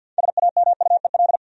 Morse code, heard as a keyed tone, about 47 wpm, 700 Hz. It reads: SAMUEL
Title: USA Common Names - 50wpm